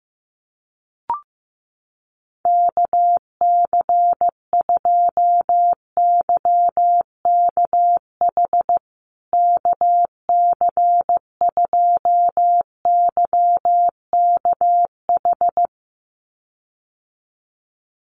Morse code, heard as a keyed tone, about 15 wpm, 700 Hz. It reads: KC2YKH KC2YKH
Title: USA Call Signs Repeated 2x 15wpm